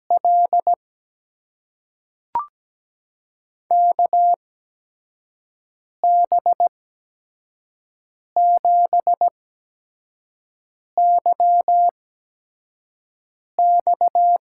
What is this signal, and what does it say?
Morse code, heard as a keyed tone, about 17 wpm, spaced as for 5 wpm, 700 Hz. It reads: L KB7YX
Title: USA Call Signs - 8x Character Spacing - 17wpm